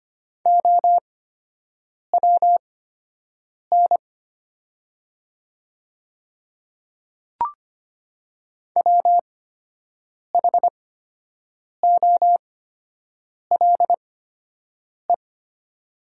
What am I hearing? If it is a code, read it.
Morse code, heard as a keyed tone, about 25 wpm, spaced as for 7 wpm, 700 Hz. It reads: OWN WHOLE
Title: 8x Character Spacing - Top 500 Words - 25wpm